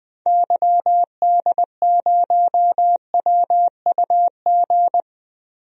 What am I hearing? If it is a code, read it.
Morse code, heard as a keyed tone, about 20 wpm, 700 Hz. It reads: YD0WUG
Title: DXCC-100 Call Signs - 6 Characters - 20wpm